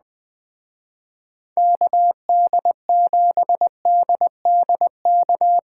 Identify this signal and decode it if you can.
Morse code, heard as a keyed tone, about 19 wpm, 700 Hz. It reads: KD7DDK
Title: Speed Racing - USA 2x3 Call Signs 17wpm